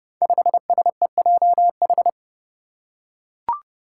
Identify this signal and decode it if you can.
Morse code, heard as a keyed tone, about 30 wpm, 700 Hz. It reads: 5SEJH
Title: Five Challenging Letter-Number Combinations 30wpm